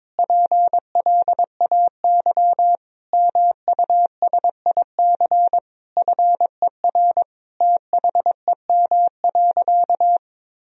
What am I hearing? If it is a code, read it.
Morse code, heard as a keyed tone, about 22 wpm, 700 Hz. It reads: PLAY MUSIC FER THEM.
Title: Sentences from Top 300 Words and Abbreviations 22wpm